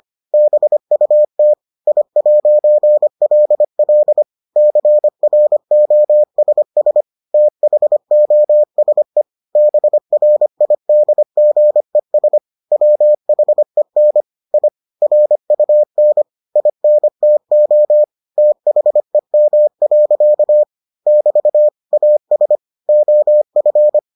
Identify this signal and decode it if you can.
Morse code, heard as a keyed tone, about 25 wpm, 600 Hz. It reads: BUT I'LL CROSS THOSE BRIDGES WHEN I RUN INTO THEM. <BT> AS OF